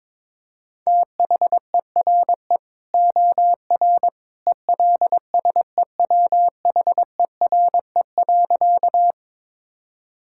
Morse code, heard as a keyed tone, about 21 wpm, 700 Hz. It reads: THERE OR ELSEWHERE.
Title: Sentences from Top 1600 Words 22wpm